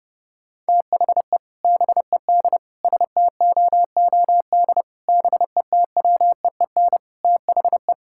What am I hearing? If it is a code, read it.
Morse code, heard as a keyed tone, about 30 wpm, 700 Hz. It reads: THE BED STOOD BETWEEN THE
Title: Sentences from Top 1600 Words 30wpm